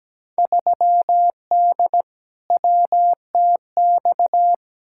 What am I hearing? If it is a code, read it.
Morse code, heard as a keyed tone, about 17 wpm, 700 Hz. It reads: 3D WTX
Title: Field Day Exchanges Encore - 17wpm